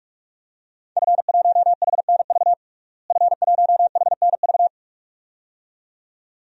Morse code, heard as a keyed tone, about 44 wpm, 700 Hz. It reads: F1HNV F1HNV
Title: DXCC-100 Call Signs - 5 Characters - Repeated 2x - 45wpm